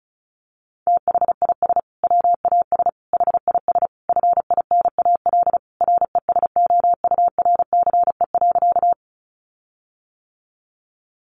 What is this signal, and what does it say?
Morse code, heard as a keyed tone, about 34 wpm, 700 Hz. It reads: THIS WAS HIS FINAL RESOURCE.
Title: Sentences from Top 1300 Words 35wpm